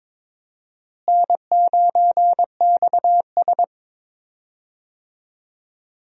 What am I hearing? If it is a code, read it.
Morse code, heard as a keyed tone, about 22 wpm, 700 Hz. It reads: N9XS
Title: USA Call Signs - 4 Characters - Repeated 3x - 22wpm